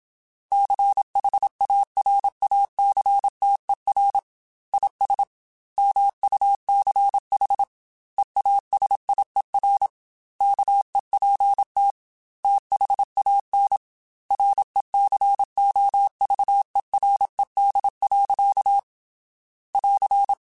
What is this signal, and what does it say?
Morse code, heard as a keyed tone, about 27 wpm, 800 Hz. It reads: CHARACTER IS MUCH EASIER KEPT THAN RECOVERED. <AR>
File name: Quotes / Quotes-mp3s / Character @25